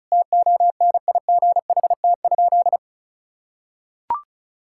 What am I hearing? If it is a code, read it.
Morse code, heard as a keyed tone, about 34 wpm, 700 Hz. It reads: TONIGHT?
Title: Sentences from Top 1600 Words 35wpm